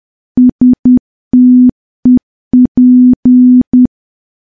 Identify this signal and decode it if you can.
Morse code, heard as a keyed tone, about 10 wpm, 250 Hz. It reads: STEP
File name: morse / morse-training-10WPM